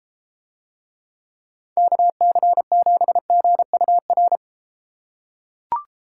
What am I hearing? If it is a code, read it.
Morse code, heard as a keyed tone, about 32 wpm, 700 Hz. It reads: KC7GUR